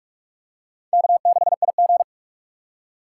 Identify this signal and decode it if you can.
Morse code, heard as a keyed tone, about 44 wpm, 700 Hz. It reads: K6IG